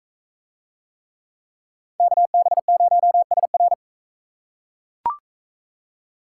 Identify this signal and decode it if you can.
Morse code, heard as a keyed tone, about 40 wpm, 700 Hz. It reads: KB0SR